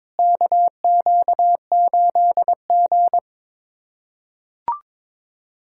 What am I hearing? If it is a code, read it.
Morse code, heard as a keyed tone, about 21 wpm, 700 Hz. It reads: KQ8G
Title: USA Callsign DE Callsign - 4 Characters - 22wpm